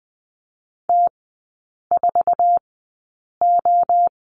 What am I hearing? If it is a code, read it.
Morse code, heard as a keyed tone, about 20 wpm, spaced as for 8 wpm, 700 Hz. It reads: T4O
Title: 047 Lesson 9e - Number 4 - Three Characters 20wpm